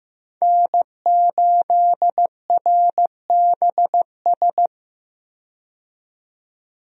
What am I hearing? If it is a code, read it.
Morse code, heard as a keyed tone, about 15 wpm, 700 Hz. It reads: N8RBS
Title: USA Call Signs - 5 Characters - 15wpm